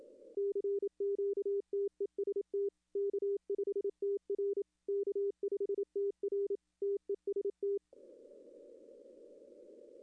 Morse code, heard as a keyed tone, about 28 wpm, 400 Hz. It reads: CQTEST K5TR K5TR TEST
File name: contests / arrl-160-k5tr-op-w5tn -2300